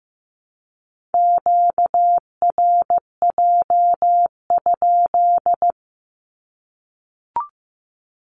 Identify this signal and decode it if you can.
Morse code, heard as a keyed tone, about 15 wpm, 700 Hz. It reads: QRJ?
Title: Q-Codes Repeated 3x 15wpm